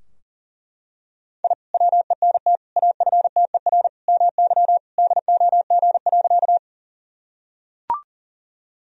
Morse code, heard as a keyed tone, about 40 wpm, 700 Hz. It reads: I WENT AFTER MY DOG.
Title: Sentences from Top 1000 Words - Review Part 2 - 40wpm